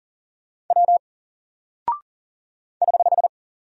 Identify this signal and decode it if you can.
Morse code, heard as a keyed tone, about 38 wpm, 700 Hz. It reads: W <HH>
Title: Single Character (Letter:Number:Punctuation:Prosign) Alt 40wpm